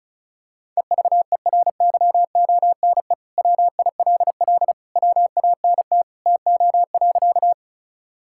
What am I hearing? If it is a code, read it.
Morse code, heard as a keyed tone, about 34 wpm, 700 Hz. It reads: EVERYONE WILL WANT TO.
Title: Sentences from Top 1000 Words - Review Part 5 - 35wpm